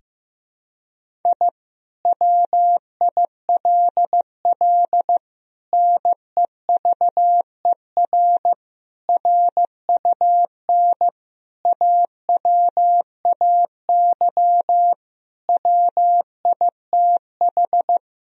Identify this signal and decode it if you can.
Morse code, heard as a keyed tone, about 16 wpm, 700 Hz. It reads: I WILL NEVER RUN AWAY WITH
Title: Sentences from Top 1000 Words - Review Part 1 - 15wpm